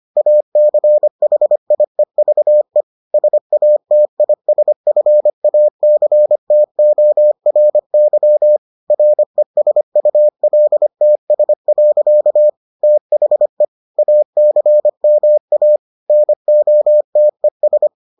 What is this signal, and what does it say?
Morse code, heard as a keyed tone, about 25 wpm, 600 Hz. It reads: ACHIEVE SATISFACTORY RESULTS. THE ACMA NOTES